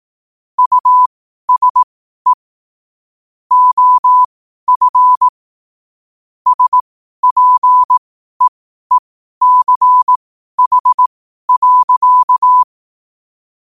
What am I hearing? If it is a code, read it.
Morse code, heard as a keyed tone, about 18 wpm, spaced as for 12 wpm, 1000 Hz. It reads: USE OF SPEECH.
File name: qotdfiles / 12wpm